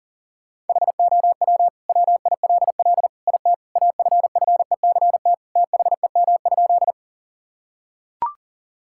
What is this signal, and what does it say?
Morse code, heard as a keyed tone, about 40 wpm, 700 Hz. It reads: HOW WILL IT AFFECT THEM?
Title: Sentences from Top 1000 Words - Review Part 6 - 40wpm